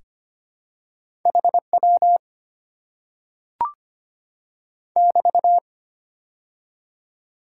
Morse code, heard as a keyed tone, about 25 wpm, 690 Hz. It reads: HW <BT>
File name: Top 90 QSO Elements - 6x Character Spacing - 25wpm